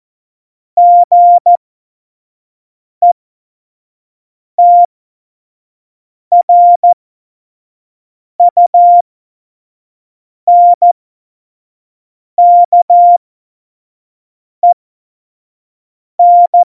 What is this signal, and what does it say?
Morse code, heard as a keyed tone, about 14 wpm, spaced as for 5 wpm, 700 Hz. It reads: GETRUNKEN
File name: Morsekurs / Witze / Witz 19 Mai 2020, 25 F70